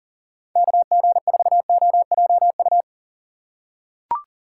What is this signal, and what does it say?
Morse code, heard as a keyed tone, about 40 wpm, 700 Hz. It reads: KG4OJU
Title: USA Call Signs - 6 Characters - 3x Character Spacing - 40wpm